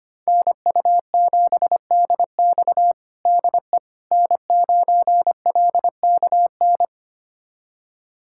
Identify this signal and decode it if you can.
Morse code, heard as a keyed tone, about 25 wpm, 700 Hz. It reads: NU7DX DE N9LKN
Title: USA Callsign DE Callsign - 5 Characters - 25wpm